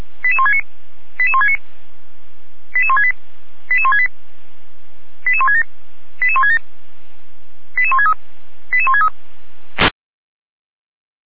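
Five-tone selective call (ZVEI), heard as quick runs of five tones, tones of about 70 ms. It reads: 88169 88169 88168 88168 88167 88167 88163 88163
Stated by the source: ZVEI five-tone selective call received on SDR.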